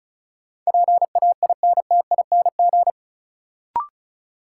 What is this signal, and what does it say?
Morse code, heard as a keyed tone, about 36 wpm, 700 Hz. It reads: PAINTING
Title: Sentences from Top 1200 Words - New Word Drill Alt - 35wpm